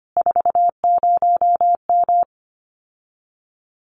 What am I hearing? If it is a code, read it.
Morse code, heard as a keyed tone, about 25 wpm, 700 Hz. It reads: 40M
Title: Common QSO Phrases Encore 25wpm